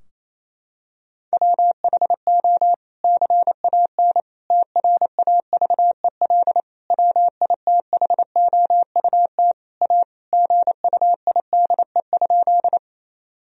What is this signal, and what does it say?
Morse code, heard as a keyed tone, about 28 wpm, 700 Hz. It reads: WHO CAN TRAVEL WITHOUT A GUIDE?
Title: Sentences from Top 1000 Words - Review Part 3 - 28wpm